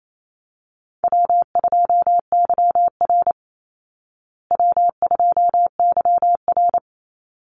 Morse code, heard as a keyed tone, about 28 wpm, 700 Hz. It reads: W2YR W2YR
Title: Easy Call Signs Repeated 2x - 28wpm